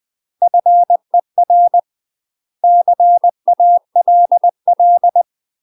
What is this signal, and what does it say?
Morse code, heard as a keyed tone, about 20 wpm, 700 Hz. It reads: FER CALL
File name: UK MASTER QSOs TRAINING / 20WPM / UK MASTER QSOs 0003